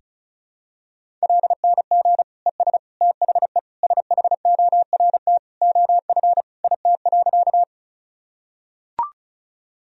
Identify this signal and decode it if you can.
Morse code, heard as a keyed tone, about 34 wpm, 700 Hz. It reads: LNG ES THE SHORT OF IT.
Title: Sentences from Top 400 Words and Abbreviations 35wpm